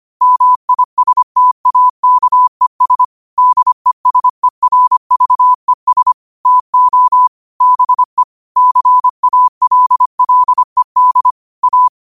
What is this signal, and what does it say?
Morse code, heard as a keyed tone, about 25 wpm, 1000 Hz. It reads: MISTAKES DESERVES TO BE CALLED A